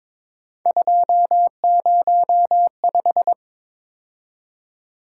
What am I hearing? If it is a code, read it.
Morse code, heard as a keyed tone, about 22 wpm, 700 Hz. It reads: 205
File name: practice / 22wpm / Three Letter-Number Combinations - 22wpm